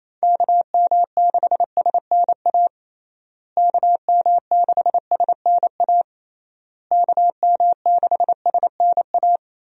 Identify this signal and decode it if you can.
Morse code, heard as a keyed tone, about 27 wpm, 700 Hz. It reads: KM6SNA KM6SNA KM6SNA
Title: USA Call Signs - 6 Characters - Repeated 3x - 28wpm